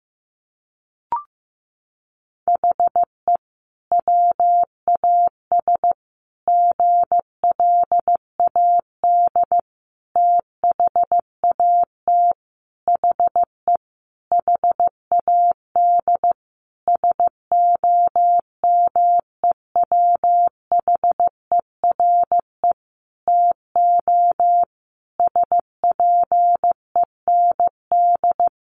Morse code, heard as a keyed tone, about 15 wpm, 700 Hz. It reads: HE WAS GLAD THAT HE HAD SOMEWHERE TO SPEND
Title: Sentences from Top 1400 Words 15wpm